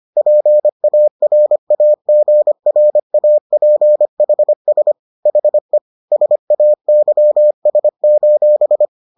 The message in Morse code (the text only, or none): PARAGRAPHS HE SAYS: